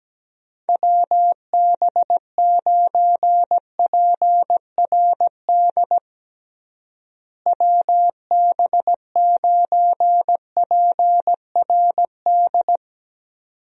Morse code, heard as a keyed tone, about 17 wpm, 700 Hz. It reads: WB9PRD WB9PRD
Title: USA Call Signs - 6 Characters - Repeated 2x - 17wpm